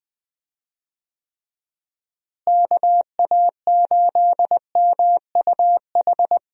Morse code, heard as a keyed tone, about 20 wpm, 700 Hz. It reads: KA8MUH
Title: USA Call Signs - 6 Characters - Repeated 3x - 20wpm